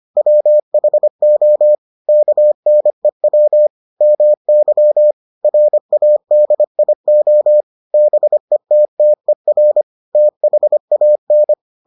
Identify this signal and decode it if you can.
Morse code, heard as a keyed tone, about 25 wpm, 600 Hz. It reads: WHO KNEW MY RADIO BETTER THAN